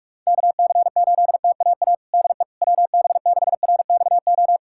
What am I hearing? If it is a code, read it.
Morse code, heard as a keyed tone, about 46 wpm, 700 Hz. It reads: KC8TAA DE WB6RXO
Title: USA Callsign DE Callsign - 6 Characters - 45wpm